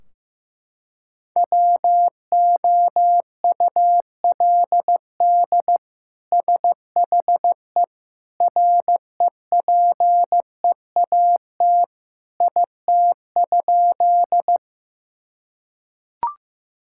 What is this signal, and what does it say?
Morse code, heard as a keyed tone, about 15 wpm, 700 Hz. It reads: WOULD SHE REPEAT IT?
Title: Sentences from Top 1600 Words 15wpm